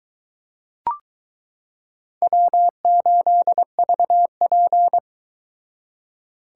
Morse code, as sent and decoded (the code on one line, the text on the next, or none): .-- ---.. ...- .--.
W8VP